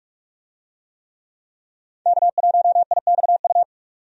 Morse code, heard as a keyed tone, about 46 wpm, 700 Hz. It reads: K1IXU
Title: USA Call Signs - 5 Characters - 45wpm